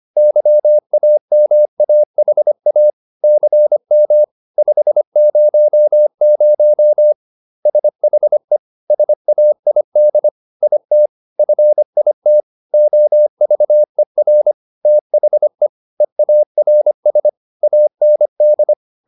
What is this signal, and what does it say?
Morse code, heard as a keyed tone, about 25 wpm, 600 Hz. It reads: YAMAHA CM 500 SHE SAID IT FIT OVER THE EARS AND